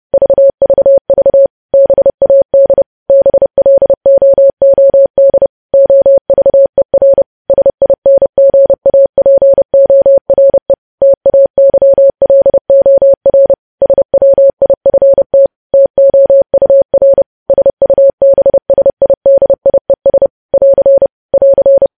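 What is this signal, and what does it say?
Morse code, heard as a keyed tone, about 30 wpm, 550 Hz. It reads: VVV BAD BLOOD OVER SINGAPORE TAYLOR SWIFT TOUR SUBSIDIES <AR> <AR>
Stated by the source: Morse code practice of news headlines at 30 wpm, preceded by vvv.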